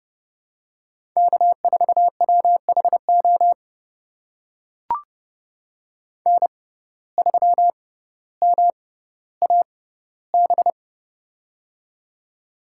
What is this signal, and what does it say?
Morse code, heard as a keyed tone, about 32 wpm, spaced as for 16 wpm, 700 Hz. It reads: K4WHO N3MAB
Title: USA Call Signs - 5 Characters - 6x Character Spacing - 30wpm